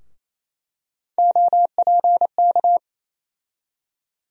Morse code, heard as a keyed tone, about 28 wpm, 700 Hz. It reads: OPK
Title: USA Call Sign Postfixes - 3 Letters 28wpm